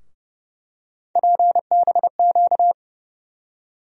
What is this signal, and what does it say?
Morse code, heard as a keyed tone, about 30 wpm, 700 Hz. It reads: PBQ
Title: USA Call Sign Postfixes - 3 Letters 30wpm